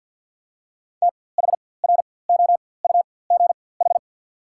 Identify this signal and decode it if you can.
Morse code, heard as a keyed tone, about 54 wpm, spaced as for 24 wpm, 700 Hz. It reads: THROUGH
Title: 4x Character Spacing - Top 300 Words - 50wpm